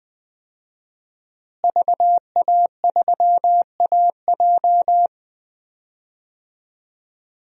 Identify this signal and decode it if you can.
Morse code, heard as a keyed tone, about 20 wpm, 700 Hz. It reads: VA3AJ
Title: DXCC-100 Call Signs - 5 Characters - Repeated 3x - 20wpm